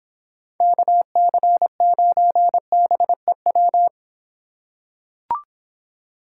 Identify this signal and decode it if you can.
Morse code, heard as a keyed tone, about 26 wpm, 700 Hz. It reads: KC9BEW